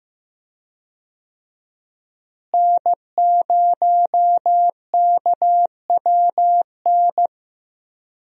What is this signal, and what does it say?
Morse code, heard as a keyed tone, about 15 wpm, 700 Hz. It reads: N0KWN
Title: USA Call Signs - 5 Characters - Repeated 2x - 15wpm